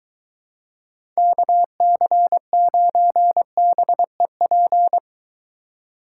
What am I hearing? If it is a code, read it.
Morse code, heard as a keyed tone, about 22 wpm, 700 Hz. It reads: KC9BEP